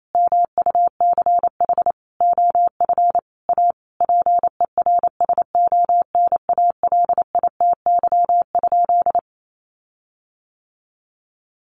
Morse code, heard as a keyed tone, about 28 wpm, 700 Hz. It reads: MUCH OF A PERSONALITY?
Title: Sentences from Top 1400 Words 28wpm